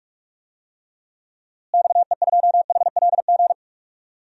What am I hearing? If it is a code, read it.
Morse code, heard as a keyed tone, about 41 wpm, 700 Hz. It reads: XE2HLG
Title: DXCC-100 Call Signs - 6 Characters - Repeated 2x - 45wpm